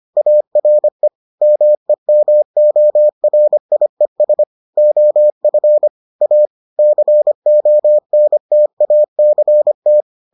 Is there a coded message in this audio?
Yes: Morse code from 0.2 to 10.0 s, heard as a keyed tone, about 25 wpm, 600 Hz. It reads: ARE MEMORIES OF A CONTACT